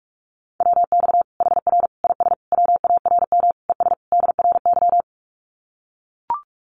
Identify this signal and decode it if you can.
Morse code, heard as a keyed tone, about 44 wpm, 700 Hz. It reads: WX HR IS WARM ES DRY
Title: Common QSO Phrases 45wpm